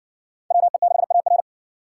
Morse code, heard as a keyed tone, about 61 wpm, 700 Hz. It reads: WE6AR